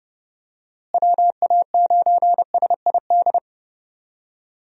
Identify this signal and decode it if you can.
Morse code, heard as a keyed tone, about 32 wpm, 700 Hz. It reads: WA9SID
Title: USA Call Signs - 6 Characters - Repeated 2x - 30wpm